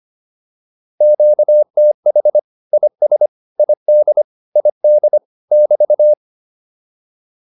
Morse code, heard as a keyed tone, about 25 wpm, 600 Hz. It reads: QTH IS ID ID <BT>
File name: QSO105 25